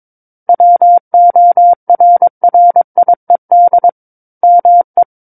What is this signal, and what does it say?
Morse code, heard as a keyed tone, about 23 wpm, 700 Hz. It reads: WORRIED ME
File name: Book One / 13-Chapter 13 - 22 WPM